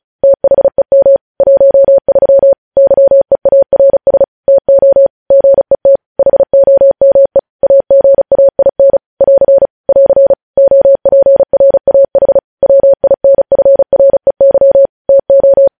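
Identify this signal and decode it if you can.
Morse code, heard as a keyed tone, about 36 wpm, 550 Hz. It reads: THEM 13 YEARS TO GET HOME AGAIN <AR> <AR> OPRAH WINFREY TO
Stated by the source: Morse code news headlines sent at 35 wpm.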